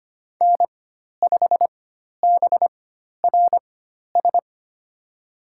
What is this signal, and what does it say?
Morse code, heard as a keyed tone, about 25 wpm, spaced as for 12 wpm, 700 Hz. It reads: N5BRS
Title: USA Call Signs - 4x Character Spacing - 25wpm